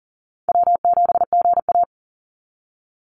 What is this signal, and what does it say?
Morse code, heard as a keyed tone, about 40 wpm, 700 Hz. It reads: W7GA
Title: NA SOTA Chaser Callsigns 40wpm